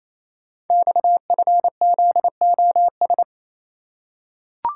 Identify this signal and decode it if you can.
Morse code, heard as a keyed tone, about 29 wpm, 700 Hz. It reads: XFZOS
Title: Code Groups - 5 Letters - 28wpm